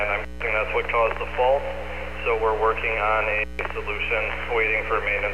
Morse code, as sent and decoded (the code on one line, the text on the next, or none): none